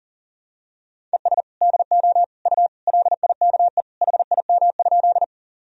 Morse code, heard as a keyed tone, about 40 wpm, 700 Hz. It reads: ES DO U LIKE HIM?
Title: Sentences from Top 100 Words and Abbreviations 40wpm